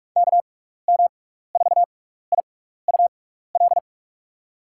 Morse code, heard as a keyed tone, about 43 wpm, spaced as for 15 wpm, 700 Hz. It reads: KM4IUL
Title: USA Call Signs - 6 Characters - 6x Character Spacing - 45wpm